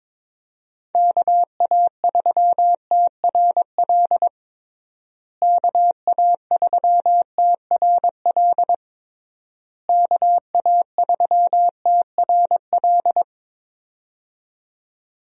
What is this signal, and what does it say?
Morse code, heard as a keyed tone, about 22 wpm, 700 Hz. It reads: KA3TRL KA3TRL KA3TRL
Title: USA Call Signs - 6 Characters - Repeated 3x - 22wpm